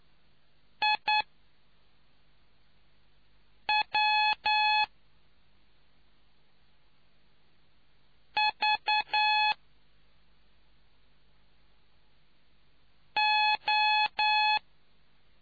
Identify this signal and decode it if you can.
Morse code, heard as a keyed tone, about 9 wpm, spaced as for 3 wpm, 850 Hz. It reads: IW V O